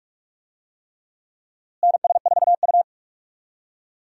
Morse code, heard as a keyed tone, about 46 wpm, 700 Hz. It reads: NS4U
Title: USA Call Signs - 4 Characters - Repeated 3x - 45wpm